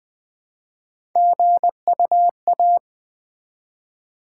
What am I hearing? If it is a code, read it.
Morse code, heard as a keyed tone, about 20 wpm, 700 Hz. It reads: GUA
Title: Build A Word - 1 Letter At A Time - 3 to 5-Letter Words - 20wpm